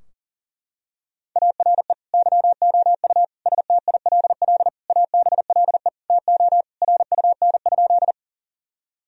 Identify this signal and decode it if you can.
Morse code, heard as a keyed tone, about 40 wpm, 700 Hz. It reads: ARE YOU STILL ABLE TO RUN?
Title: Sentences from Top 1000 Words - Review Part 2 - 40wpm